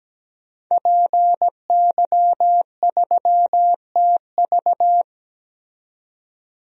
Morse code, heard as a keyed tone, about 17 wpm, 700 Hz. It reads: PY3TV
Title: DXCC-100 Call Signs - 5 Characters - 17wpm